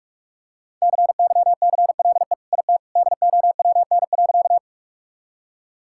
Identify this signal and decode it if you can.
Morse code, heard as a keyed tone, about 46 wpm, 700 Hz. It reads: CYCLE IT DOWN.
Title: Sentences from Top 1000 Words - Review Part 6 - 45wpm